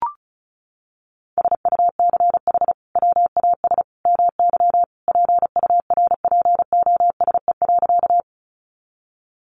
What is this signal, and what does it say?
Morse code, heard as a keyed tone, about 34 wpm, 700 Hz. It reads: SUCH WAS MY PURPOSE.
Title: Sentences from Top 800 Words 35wpm